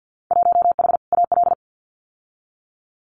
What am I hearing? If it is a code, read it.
Morse code, heard as a keyed tone, about 50 wpm, 700 Hz. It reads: 1H AL
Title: Winter Field Day Exchanges 50wpm